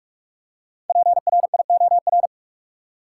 Morse code, heard as a keyed tone, about 43 wpm, 700 Hz. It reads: PRIOR
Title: Five Letter Words - 45wpm